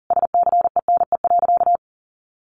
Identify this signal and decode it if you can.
Morse code, heard as a keyed tone, about 40 wpm, 700 Hz. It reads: SCENE.
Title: Sentences from Top 1300 Words 40wpm